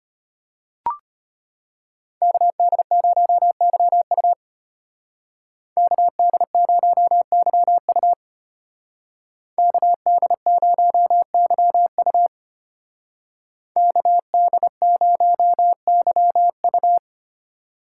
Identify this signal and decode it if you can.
Morse code, heard as a keyed tone, about 33 wpm, 700 Hz. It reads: KD0YU KD0YU KD0YU KD0YU